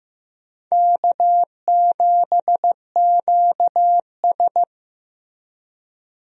Morse code, heard as a keyed tone, about 16 wpm, 700 Hz. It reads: K7QS
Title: USA Call Signs - 4 Characters - Repeated 3x - 15wpm